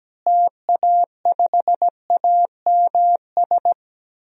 Morse code, heard as a keyed tone, about 17 wpm, 700 Hz. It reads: TA5AMS
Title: DXCC-100 Call Signs - 6 Characters - Repeated 3x - 17wpm